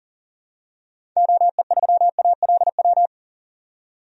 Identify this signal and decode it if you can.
Morse code, heard as a keyed tone, about 40 wpm, 700 Hz. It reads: OE3ALW
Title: DXCC-100 Call Signs - 6 Characters - 40wpm